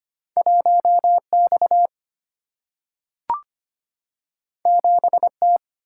Morse code, heard as a keyed tone, about 25 wpm, 700 Hz. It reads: 1X 7T
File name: Two Letter-Number Combinations - 25wpm